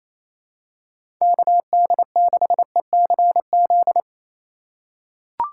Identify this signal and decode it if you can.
Morse code, heard as a keyed tone, about 28 wpm, 700 Hz. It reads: KD6ECZ
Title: USA Call Signs - 6 Characters - 8x Character Spacing - 28wpm